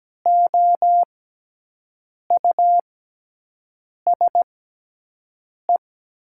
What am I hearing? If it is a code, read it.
Morse code, heard as a keyed tone, about 17 wpm, spaced as for 6 wpm, 700 Hz. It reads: OUSE
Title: 6x Character Spacing - Top 200 Words - 17wpm